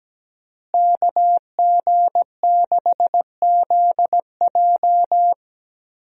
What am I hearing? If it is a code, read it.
Morse code, heard as a keyed tone, about 17 wpm, 700 Hz. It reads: KG6ZJ
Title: USA Call Signs - 5 Characters - Repeated 3x - 17wpm